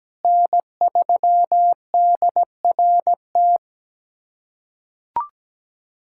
Morse code, heard as a keyed tone, about 17 wpm, 700 Hz. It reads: N3DRT
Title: USA Callsign DE Callsign - 5 Characters - 17wpm